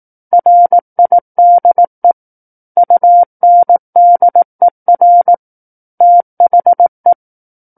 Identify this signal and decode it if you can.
Morse code, heard as a keyed tone, about 18 wpm, 700 Hz. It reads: RIDE UNDER THE
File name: 18 WPM